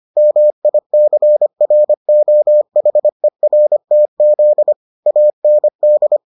Morse code, heard as a keyed tone, about 25 wpm, 600 Hz. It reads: MICROHERTZ AND